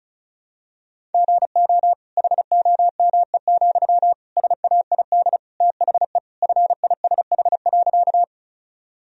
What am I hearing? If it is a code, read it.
Morse code, heard as a keyed tone, about 34 wpm, 700 Hz. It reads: GO HOME, SAID THE FISH.
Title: Sentences from Top 1000 Words - Review Part 1 - 35wpm